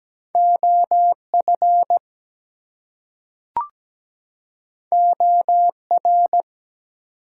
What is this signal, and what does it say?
Morse code, heard as a keyed tone, about 17 wpm, 700 Hz. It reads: OF OR
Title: Two Letter Words - 17wpm